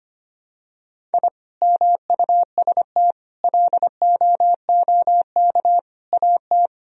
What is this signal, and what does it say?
Morse code, heard as a keyed tone, about 25 wpm, 700 Hz. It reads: I MUST LOOK AT
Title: Sentences from Top 1000 Words - Review Part 3 - 25wpm